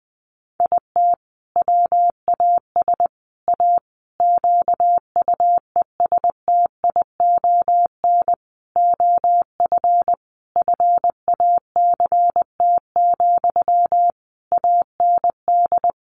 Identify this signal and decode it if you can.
Morse code, heard as a keyed tone, about 20 wpm, 700 Hz. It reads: IT WAS A QUESTION OF FACT, AND
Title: Sentences from Top 1400 Words 20wpm